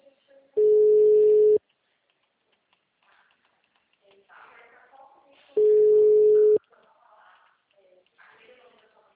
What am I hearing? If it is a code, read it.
Ringback tone (Europe, 425 Hz), with speech.